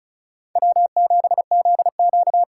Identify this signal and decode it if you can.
Morse code, heard as a keyed tone, about 36 wpm, 700 Hz. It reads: W7ZQ
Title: USA Call Signs - 2x Character Spacing - 35wpm